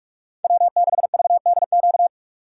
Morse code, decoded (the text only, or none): W6VDQ